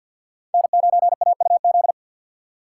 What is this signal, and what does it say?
Morse code, heard as a keyed tone, about 52 wpm, 700 Hz. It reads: N9AUZ
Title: USA Call Signs - 4x Character Spacing - 50wpm